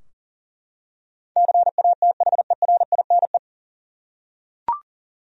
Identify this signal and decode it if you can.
Morse code, heard as a keyed tone, about 40 wpm, 700 Hz. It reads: CATHERINE
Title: USA Common Names - 40wpm